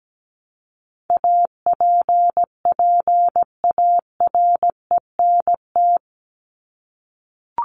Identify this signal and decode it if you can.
Morse code, heard as a keyed tone, about 17 wpm, 700 Hz. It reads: APPARENT
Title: LY Suffix - Top 100 Words with Base Words 17wpm